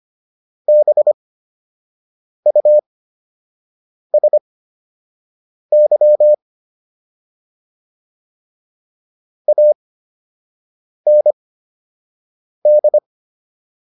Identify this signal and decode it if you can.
Morse code, heard as a keyed tone, about 25 wpm, spaced as for 6 wpm, 600 Hz. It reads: BUSY AND